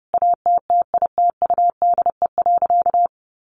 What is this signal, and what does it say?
Morse code, heard as a keyed tone, about 30 wpm, 700 Hz. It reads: ATTITUDE.
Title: Sentences from Top 1300 Words 30wpm